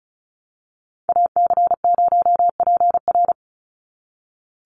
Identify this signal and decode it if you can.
Morse code, heard as a keyed tone, about 34 wpm, 700 Hz. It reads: AC0PR
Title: NA SOTA Activator Callsigns 35wpm